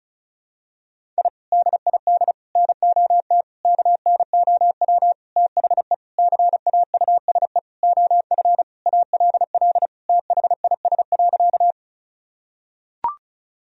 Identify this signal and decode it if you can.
Morse code, heard as a keyed tone, about 36 wpm, 700 Hz. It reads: I DID NOT KNOW THE CAUSE OF ALL THIS.
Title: Sentences from Top 1000 Words - Review Part 1 - 35wpm